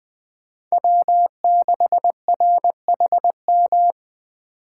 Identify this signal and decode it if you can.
Morse code, heard as a keyed tone, about 20 wpm, 700 Hz. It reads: W6RHM
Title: Speed Racing - USA 1x3 Call Signs 17wpm